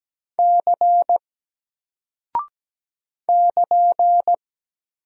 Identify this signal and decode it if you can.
Morse code, heard as a keyed tone, about 17 wpm, 700 Hz. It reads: C <KN>
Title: Single Character (Letter:Number:Punctuation:Prosign) Alt 17wpm